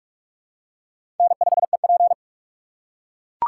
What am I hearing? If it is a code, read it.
Morse code, heard as a keyed tone, about 44 wpm, 700 Hz. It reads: N5EP
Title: USA Call Signs - 4 Characters - 6x Character Spacing - 45wpm